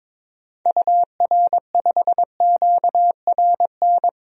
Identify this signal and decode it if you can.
Morse code, heard as a keyed tone, about 22 wpm, 700 Hz. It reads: UR5QRN
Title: DXCC-100 Call Signs - 6 Characters - 22wpm